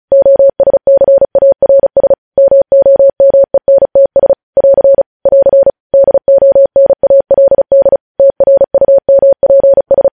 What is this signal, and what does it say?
Morse code, heard as a keyed tone, about 36 wpm, 550 Hz. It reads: OSCARS MOMENTS <AR> <AR> DONALD TRUMPS
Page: News Headlines in Morse Code at 35 WPM